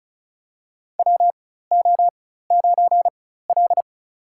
Morse code, heard as a keyed tone, about 34 wpm, spaced as for 16 wpm, 700 Hz. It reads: WO9L
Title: USA Call Signs - 4 Characters - 4x Character Spacing - 35wpm